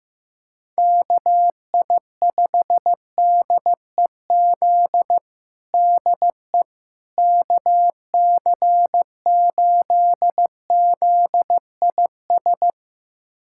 Morse code, heard as a keyed tone, about 15 wpm, 700 Hz. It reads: KI5DEZ DE KC8ZIS
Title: USA Callsign DE Callsign 15wpm